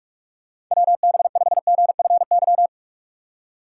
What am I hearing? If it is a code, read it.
Morse code, heard as a keyed tone, about 43 wpm, 700 Hz. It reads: WB5GFY
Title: USA Call Signs - 3x Character Spacing - 45wpm